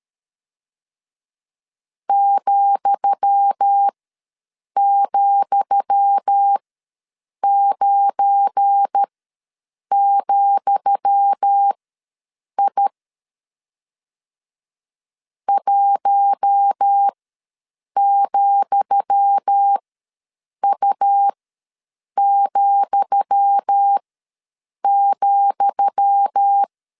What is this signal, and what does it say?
Morse code, heard as a keyed tone, about 13 wpm, spaced as for 7 wpm, 800 Hz. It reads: ,,9,I 1,U,,
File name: LEZIONI / 10 SIGNS / 10D COMA ws 12wpm 3618